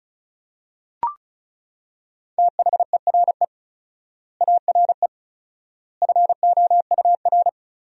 Morse code, heard as a keyed tone, about 36 wpm, 700 Hz. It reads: THERE ARE FOUR
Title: 4x Word-Spacing - Sentences from Top 400 Words - 35wpm